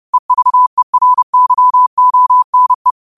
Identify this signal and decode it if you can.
Morse code, heard as a keyed tone, about 30 wpm, 1000 Hz. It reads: EVERYONE
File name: qotdfiles / 30wpm